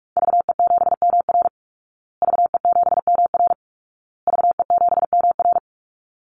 Morse code, heard as a keyed tone, about 44 wpm, 700 Hz. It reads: VE7MR VE7MR VE7MR
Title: Call Signs Repeated 2x - 45wpm